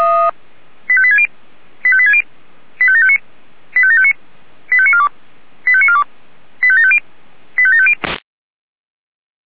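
Five-tone selective call (ZVEI), heard as quick runs of five tones, tones of about 70 ms. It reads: 85860 85860 85759 85759 85942 85942 85750 85750
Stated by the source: ZVEI five-tone call 85730 received by SDR.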